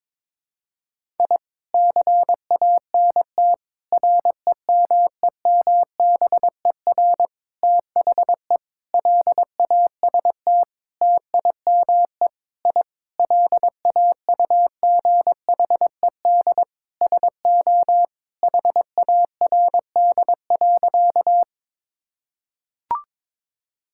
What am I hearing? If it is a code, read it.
Morse code, heard as a keyed tone, about 22 wpm, 700 Hz. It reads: I CANT REMEMBER THE LAST TIME I LAUGHED SO HARD.
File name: Sentences with 7-Word Phrases - 22wpm